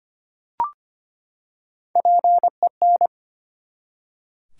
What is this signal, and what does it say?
Morse code, heard as a keyed tone, about 25 wpm, 700 Hz. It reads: PEN
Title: Three Letter Words - 25wpm